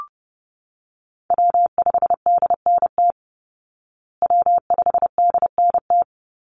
Morse code, heard as a keyed tone, about 30 wpm, 700 Hz. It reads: W5DNT W5DNT
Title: Call Signs Repeated 3x - 30wpm